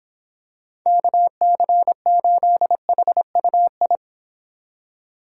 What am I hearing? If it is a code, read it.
Morse code, heard as a keyed tone, about 26 wpm, 700 Hz. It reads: KC8HUI